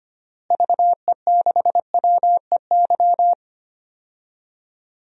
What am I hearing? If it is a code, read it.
Morse code, heard as a keyed tone, about 25 wpm, 700 Hz. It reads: VE6WEY
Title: DXCC-100 Call Signs - 6 Characters - Repeated 2x - 25wpm